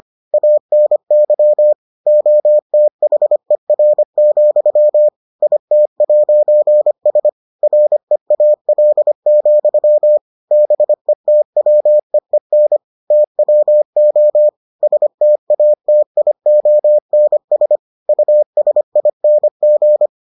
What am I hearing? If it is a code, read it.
Morse code, heard as a keyed tone, about 25 wpm, 600 Hz. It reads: ANY OTHER, IT'S REAL, BETWEEN TWO STATIONS USING